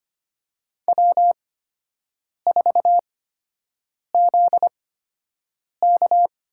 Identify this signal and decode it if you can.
Morse code, heard as a keyed tone, about 25 wpm, spaced as for 7 wpm, 700 Hz. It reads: W4ZK
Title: USA Call Signs - 8x Character Spacing - 25wpm